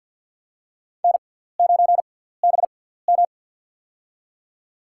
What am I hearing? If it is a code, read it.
Morse code, heard as a keyed tone, about 47 wpm, spaced as for 17 wpm, 700 Hz. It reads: N9BM
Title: USA Call Signs - 4 Characters - 6x Character Spacing - 50wpm